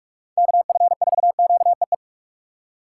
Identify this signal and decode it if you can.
Morse code, heard as a keyed tone, about 46 wpm, 700 Hz. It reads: KF4QEE
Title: USA Call Signs - 6 Characters - 8x Character Spacing - 45wpm